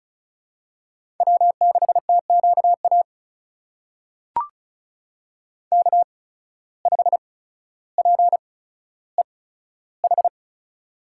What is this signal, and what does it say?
Morse code, heard as a keyed tone, about 36 wpm, spaced as for 15 wpm, 700 Hz. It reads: W6TQA K5PEH
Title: USA Call Signs - 5 Characters - 8x Character Spacing - 35wpm